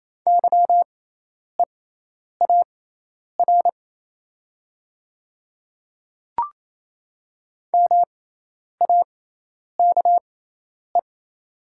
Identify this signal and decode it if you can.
Morse code, heard as a keyed tone, about 28 wpm, spaced as for 10 wpm, 700 Hz. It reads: YEAR MAKE
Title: 6x Character Spacing - Top 200 Words - 28wpm